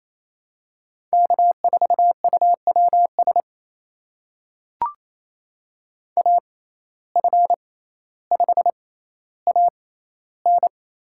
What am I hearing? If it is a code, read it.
Morse code, heard as a keyed tone, about 28 wpm, spaced as for 14 wpm, 700 Hz. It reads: K4UWS AF5AN
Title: USA Call Signs - 5 Characters - 6x Character Spacing - 28wpm